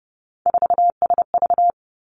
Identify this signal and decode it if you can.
Morse code, heard as a keyed tone, about 30 wpm, 700 Hz. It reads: 4SV
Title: Three Challenging Letter-Number Combinations 30wpm